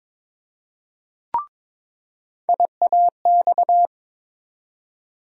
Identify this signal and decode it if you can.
Morse code, heard as a keyed tone, about 22 wpm, 700 Hz. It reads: IAX
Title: USA Call Sign Postfixes - 3 Letters 22wpm